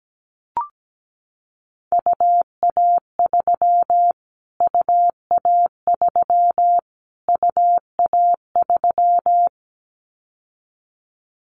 Morse code, heard as a keyed tone, about 17 wpm, 700 Hz. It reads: UA3 UA3 UA3
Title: Callsign Prefixes for Top 25 DXCC - Repeated 3x - 17wpm